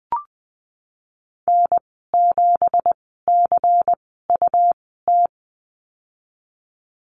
Morse code, heard as a keyed tone, about 20 wpm, spaced as for 15 wpm, 700 Hz. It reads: N7CUT
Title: 173 Lesson 28g - Number 9 - Call Signs 20wpm